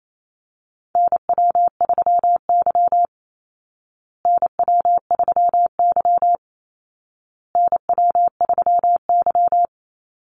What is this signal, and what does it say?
Morse code, heard as a keyed tone, about 28 wpm, 700 Hz. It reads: NW3Y NW3Y NW3Y
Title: Call Signs Repeated 2x - 28wpm